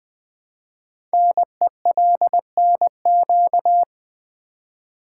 Morse code, heard as a keyed tone, about 20 wpm, 700 Hz. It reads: NELNQ
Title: Code Groups - 5 Letters - 20wpm